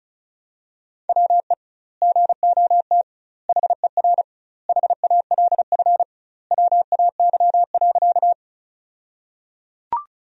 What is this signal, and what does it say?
Morse code, heard as a keyed tone, about 36 wpm, 700 Hz. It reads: WE GOT HER HALF WAY.
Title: 2x Word-Spacing - Sentences from Top 400 Words - 35wpm